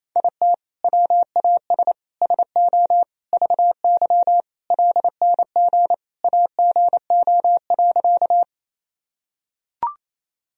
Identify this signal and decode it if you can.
Morse code, heard as a keyed tone, about 28 wpm, 700 Hz. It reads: IT WAS SO VY LNG AGO.
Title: Sentences from Top 500 Words and Abbreviations 28wpm